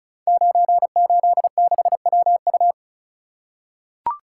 Morse code, heard as a keyed tone, about 36 wpm, 700 Hz. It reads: 986WU
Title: Code Groups - 5 Letter:Numbers - 35wpm